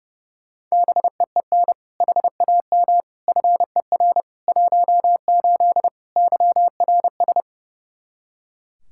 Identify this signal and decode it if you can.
Morse code, heard as a keyed tone, about 30 wpm, 700 Hz. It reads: BEEN HAM FER 18 YRS
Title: Practice QSO - Exchange 3 - More Info - 30wpm